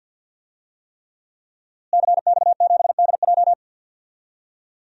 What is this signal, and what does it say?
Morse code, heard as a keyed tone, about 52 wpm, 700 Hz. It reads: CX7DJ
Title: DXCC-100 Call Signs - 5 Characters - Repeated 3x - 50wpm